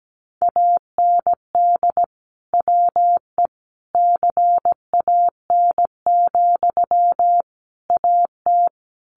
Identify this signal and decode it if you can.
Morse code, heard as a keyed tone, about 17 wpm, 700 Hz. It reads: AND WE CAN, AT
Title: Sentences from Top 900 Words 17wpm